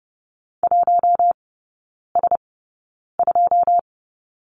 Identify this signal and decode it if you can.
Morse code, heard as a keyed tone, about 30 wpm, spaced as for 9 wpm, 700 Hz. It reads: 1S2
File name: 081 Lesson 14e - Number 2 - Three Characters 30wpm